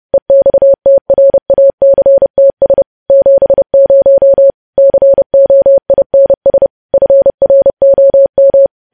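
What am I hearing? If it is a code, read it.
Morse code, heard as a keyed tone, about 30 wpm, 550 Hz. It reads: EXTRACTS 70 COINS FROM
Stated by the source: News headlines sent in Morse at 30.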